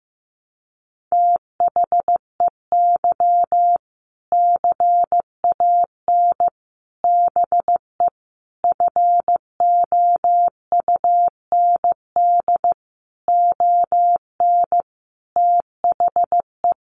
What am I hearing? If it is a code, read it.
Morse code, heard as a keyed tone, about 14 wpm, 700 Hz. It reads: THEY CAN BE FOUND ON THE
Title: Sentences from Top 1500 Words 15wpm